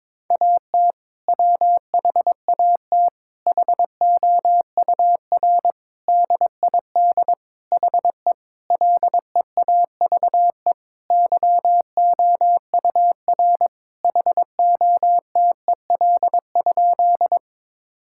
Morse code, heard as a keyed tone, about 22 wpm, 700 Hz. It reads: AT WHAT HOUR DID HE LEAVE YOUR HOTEL?
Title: Sentences from Top 1600 Words 22wpm